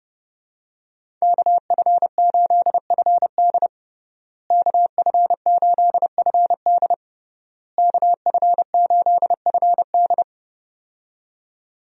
Morse code, heard as a keyed tone, about 30 wpm, 700 Hz. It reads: KF8FD KF8FD KF8FD
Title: USA Call Signs - 5 Characters - Repeated 3x - 30wpm